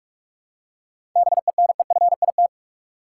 Morse code, heard as a keyed tone, about 43 wpm, 700 Hz. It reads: BENEFIT
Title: Seven Letter Words - 45wpm